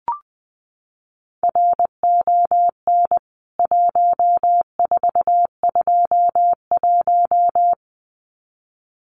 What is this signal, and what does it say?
Morse code, heard as a keyed tone, about 20 wpm, 700 Hz. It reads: RON 1421
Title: CWT Exchange 20wpm